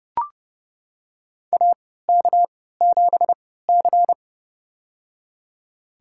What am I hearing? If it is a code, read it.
Morse code, heard as a keyed tone, about 31 wpm, spaced as for 17 wpm, 700 Hz. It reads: AK7C
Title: USA Call Signs - 4 Characters - 3x Character Spacing - 30wpm